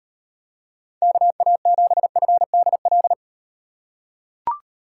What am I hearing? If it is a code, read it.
Morse code, heard as a keyed tone, about 38 wpm, 700 Hz. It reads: KA7FDL